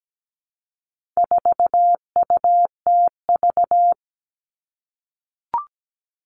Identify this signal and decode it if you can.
Morse code, heard as a keyed tone, about 17 wpm, 700 Hz. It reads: 4UTV
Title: All Cut Numbers - 4-Digit 17wpm